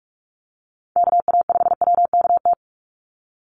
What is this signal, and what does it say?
Morse code, heard as a keyed tone, about 44 wpm, 700 Hz. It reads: KA5WKT
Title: USA Call Signs 45wpm